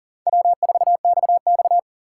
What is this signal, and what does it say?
Morse code, heard as a keyed tone, about 40 wpm, 700 Hz. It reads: W4XX
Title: USA Call Signs - 3x Character Spacing - 40wpm